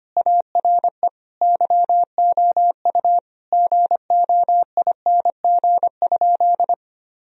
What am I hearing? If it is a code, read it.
Morse code, heard as a keyed tone, about 25 wpm, 700 Hz. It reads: ARE YOU GOING?
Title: Sentences from Top 1000 Words - Review Part 2 - 25wpm